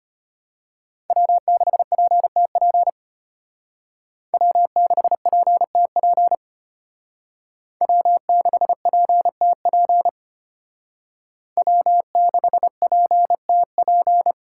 Morse code, heard as a keyed tone, about 34 wpm, 700 Hz. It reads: W6PTP W6PTP W6PTP W6PTP